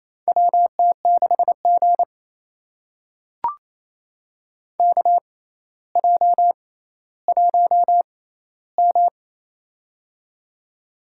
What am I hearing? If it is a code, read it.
Morse code, heard as a keyed tone, about 28 wpm, spaced as for 14 wpm, 700 Hz. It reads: WT6G KJ1M
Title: USA Call Signs - 4 Characters - 6x Character Spacing - 28wpm